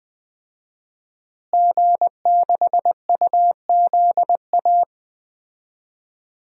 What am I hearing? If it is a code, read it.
Morse code, heard as a keyed tone, about 20 wpm, 700 Hz. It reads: G6UZA
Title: DXCC-100 Call Signs - 5 Characters - Repeated 2x - 20wpm